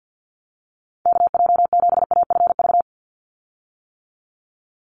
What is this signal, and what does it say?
Morse code, heard as a keyed tone, about 50 wpm, 700 Hz. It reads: KJ7AFV
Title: USA Call Signs 50wpm